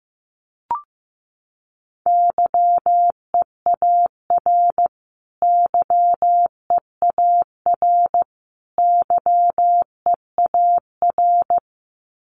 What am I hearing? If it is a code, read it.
Morse code, heard as a keyed tone, about 15 wpm, 700 Hz. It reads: YEAR YEAR YEAR
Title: Top 200 Words Repeated 15wpm